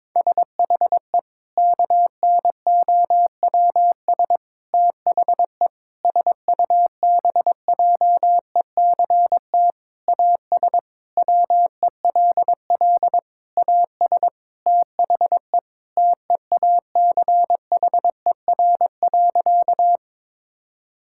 Morse code, heard as a keyed tone, about 22 wpm, 700 Hz. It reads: SHE KNOWS THE SUBJECT AS WELL AS THE TEACHER.
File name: practice / 22wpm / Sentences with 4-Word Phrases - 22wpm